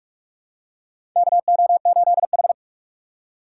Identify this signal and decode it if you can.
Morse code, heard as a keyed tone, about 46 wpm, 700 Hz. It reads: KO8H
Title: USA Call Signs - 4 Characters - 3x Character Spacing - 45wpm